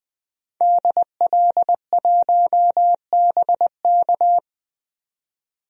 Morse code, heard as a keyed tone, about 21 wpm, 700 Hz. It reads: DL1BK
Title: DXCC-100 Call Signs - 5 Characters - 20wpm